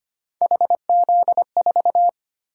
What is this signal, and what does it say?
Morse code, heard as a keyed tone, about 25 wpm, 700 Hz. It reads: HZ4
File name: Three Letter-Number Combinations - 25wpm